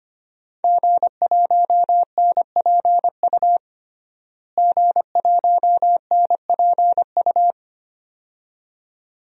Morse code, heard as a keyed tone, about 25 wpm, 700 Hz. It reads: G1NPU G1NPU
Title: DXCC-100 Call Signs - 5 Characters - Repeated 2x - 25wpm